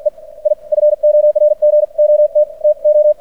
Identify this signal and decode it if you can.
Morse code, heard as a keyed tone, about 56 wpm, spaced as for 46 wpm, 600 Hz. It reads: E I FORGOT TO
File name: archived / CRSnet